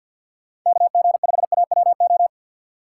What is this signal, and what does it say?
Morse code, heard as a keyed tone, about 51 wpm, 700 Hz. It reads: KG5AWO
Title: USA Call Signs - 6 Characters - 6x Character Spacing - 50wpm